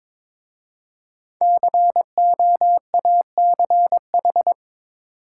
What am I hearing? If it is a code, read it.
Morse code, heard as a keyed tone, about 22 wpm, 700 Hz. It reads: COACH
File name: 5 Letter Words - First 3 Letters Given - 22wpm